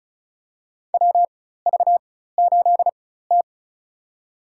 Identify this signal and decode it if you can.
Morse code, heard as a keyed tone, about 34 wpm, spaced as for 16 wpm, 700 Hz. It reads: WV8T
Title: USA Call Signs - 4 Characters - 4x Character Spacing - 35wpm